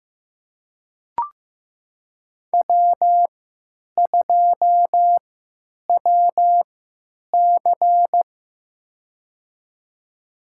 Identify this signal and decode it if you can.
Morse code, heard as a keyed tone, about 15 wpm, spaced as for 9 wpm, 700 Hz. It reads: W2WC
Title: USA Call Signs - 4 Characters - 3x Character Spacing - 15wpm